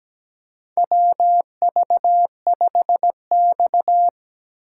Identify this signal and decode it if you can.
Morse code, heard as a keyed tone, about 17 wpm, 700 Hz. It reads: WV5X
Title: USA Call Signs - 4x Character Spacing - 17wpm